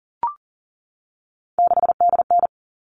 Morse code, heard as a keyed tone, about 40 wpm, 700 Hz. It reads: 6DN
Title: All Cut Numbers - 3-Digit 40wpm